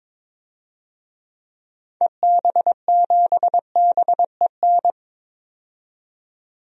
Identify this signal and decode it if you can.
Morse code, heard as a keyed tone, about 22 wpm, 700 Hz. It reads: EB7BEN
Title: DXCC-100 Call Signs - 6 Characters - 22wpm